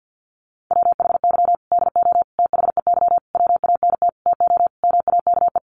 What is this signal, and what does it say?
Morse code, heard as a keyed tone, about 50 wpm, 700 Hz. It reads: WHY DO THEY WANT TO MAKE
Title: Sentences from Top 400 Words 50wpm